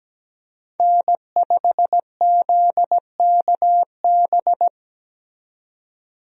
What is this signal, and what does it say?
Morse code, heard as a keyed tone, about 17 wpm, 700 Hz. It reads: N5ZKB
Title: USA Call Signs - 5 Characters - 17wpm